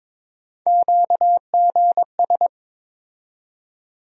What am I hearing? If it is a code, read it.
Morse code, heard as a keyed tone, about 22 wpm, 700 Hz. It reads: QGS
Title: USA Call Sign Postfixes - 3 Letters 22wpm